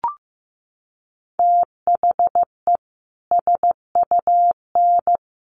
Morse code, heard as a keyed tone, about 15 wpm, 700 Hz. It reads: THE SUN
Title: A Sound Of Thunder 15wpm